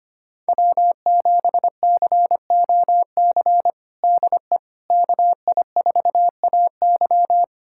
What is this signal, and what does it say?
Morse code, heard as a keyed tone, about 25 wpm, 700 Hz. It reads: W7COC DE KI4AY
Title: USA Callsign DE Callsign - 5 Characters - 25wpm